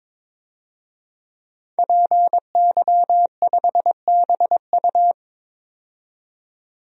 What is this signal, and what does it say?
Morse code, heard as a keyed tone, about 22 wpm, 700 Hz. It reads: PY5BU
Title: DXCC-100 Call Signs - 5 Characters - 22wpm